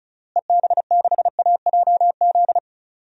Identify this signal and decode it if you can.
Morse code, heard as a keyed tone, about 36 wpm, 700 Hz. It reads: EB6AJZ
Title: DXCC-100 Call Signs - 6 Characters - Repeated 3x - 35wpm